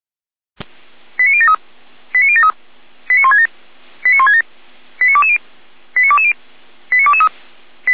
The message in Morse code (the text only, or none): none